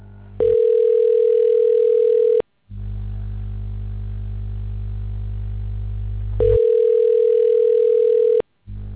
Ringback tone (North America), 440 + 480 Hz.